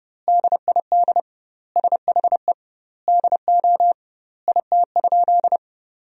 Morse code, heard as a keyed tone, about 32 wpm, 700 Hz. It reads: DID SHE DO IT?
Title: 2x Word-Spacing - Sentences from Top 100 Words - 30wpm